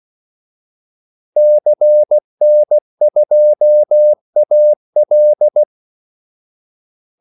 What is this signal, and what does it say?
Morse code, heard as a keyed tone, about 16 wpm, 600 Hz. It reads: CN2AL
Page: Kaldesignaler 16 wpm